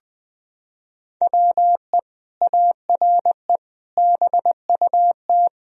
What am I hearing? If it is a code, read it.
Morse code, heard as a keyed tone, about 20 wpm, 700 Hz. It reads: WE ARE BUT
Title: Sentences from Top 1000 Words - Review Part 3 - 20wpm